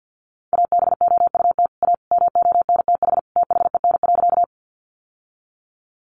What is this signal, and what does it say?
Morse code, heard as a keyed tone, about 50 wpm, 700 Hz. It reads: ABOUT A MONTH THEN.
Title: Sentences from Top 800 Words 50wpm